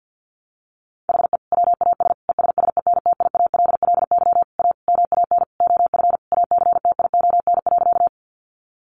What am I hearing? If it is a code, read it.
Morse code, heard as a keyed tone, about 50 wpm, 700 Hz. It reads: HE WAS ESSENTIALLY A MAN OF ACTION.
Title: Sentences from Top 1300 Words 50wpm